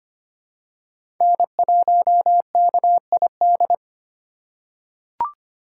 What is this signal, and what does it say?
Morse code, heard as a keyed tone, about 25 wpm, 700 Hz. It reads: N1KID
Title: USA Call Signs - 2x Character Spacing - 25wpm